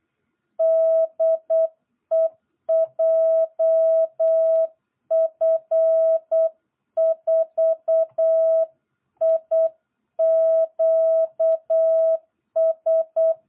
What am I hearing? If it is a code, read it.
Morse code, heard as a keyed tone, about 8 wpm, 660 Hz. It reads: DE JF4IQS
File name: jh8cbh / 2022cw